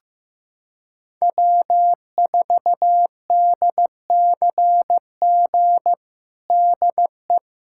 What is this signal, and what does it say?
Morse code, heard as a keyed tone, about 15 wpm, 700 Hz. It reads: W4DCG DE
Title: USA Callsign DE Callsign - 5 Characters - 15wpm